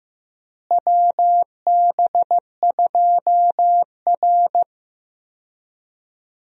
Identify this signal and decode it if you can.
Morse code, heard as a keyed tone, about 15 wpm, 700 Hz. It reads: WB2R
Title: USA Call Signs - 4 Characters - Repeated 3x - 15wpm